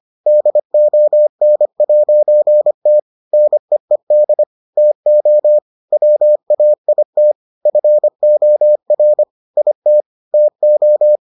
Morse code, heard as a keyed tone, about 25 wpm, 600 Hz. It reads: DON'T NEED TO WAIT FOR IT TO